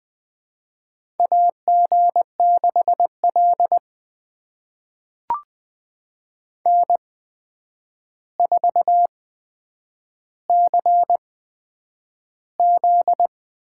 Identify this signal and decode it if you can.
Morse code, heard as a keyed tone, about 20 wpm, spaced as for 9 wpm, 700 Hz. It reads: AG6L N4CZ
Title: USA Call Signs - 4 Characters - 8x Character Spacing - 20wpm